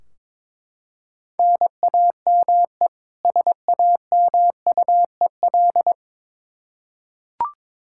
Morse code, heard as a keyed tone, about 22 wpm, 700 Hz. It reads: NAME SAMUEL
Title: Practice QSO - Exchange 1 - Beginning - 22wpm